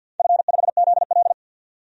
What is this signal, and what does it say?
Morse code, heard as a keyed tone, about 48 wpm, 700 Hz. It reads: F5ZL
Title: DXCC-100 Call Signs - 4 Characters - 50wpm